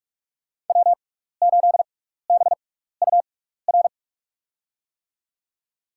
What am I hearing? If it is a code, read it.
Morse code, heard as a keyed tone, about 42 wpm, spaced as for 15 wpm, 700 Hz. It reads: W8BUR
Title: USA Call Signs - 6x Character Spacing - 45wpm